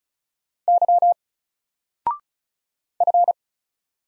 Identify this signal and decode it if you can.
Morse code, heard as a keyed tone, about 36 wpm, 700 Hz. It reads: Y F
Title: Single Character (Letter:Number:Punctuation:Prosign) 35wpm